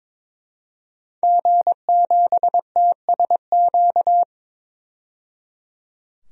Morse code, heard as a keyed tone, about 22 wpm, 700 Hz. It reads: G7TSQ
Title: DXCC-100 Call Signs - 5 Characters - 22wpm